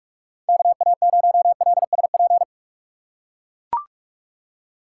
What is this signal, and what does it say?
Morse code, heard as a keyed tone, about 44 wpm, 700 Hz. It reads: KA0LSP
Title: USA Call Signs - 6 Characters - 2x Character Spacing - 45wpm